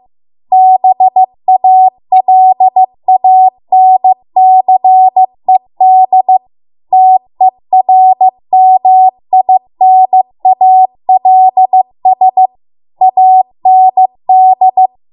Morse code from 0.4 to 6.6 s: BALANCED